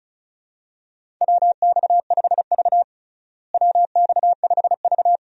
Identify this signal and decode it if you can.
Morse code, heard as a keyed tone, about 36 wpm, 700 Hz. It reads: WX5V WX5V
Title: USA Call Signs - 4 Characters - Repeated 2x - 35wpm